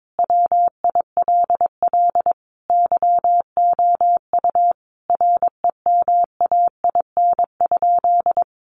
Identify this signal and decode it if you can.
Morse code, heard as a keyed tone, about 22 wpm, 700 Hz. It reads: WILL YOU REMAIN?
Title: Sentences from Top 1200 Words 22wpm